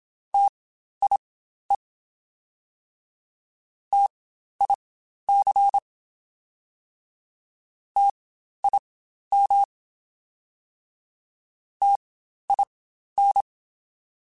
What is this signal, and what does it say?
Morse code, heard as a keyed tone, about 27 wpm, spaced as for 12 wpm, 800 Hz. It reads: TIE TIC TIM TIN
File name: Spaced Alphas / Spaced Alphas mp3s / Ts sp @25